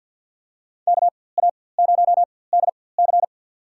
Morse code, heard as a keyed tone, about 52 wpm, spaced as for 24 wpm, 700 Hz. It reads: KA0DC
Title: USA Call Signs - 5 Characters - 4x Character Spacing - 50wpm